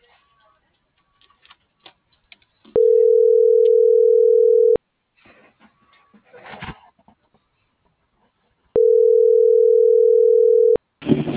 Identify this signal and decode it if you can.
Ringback tone (North America), 440 + 480 Hz.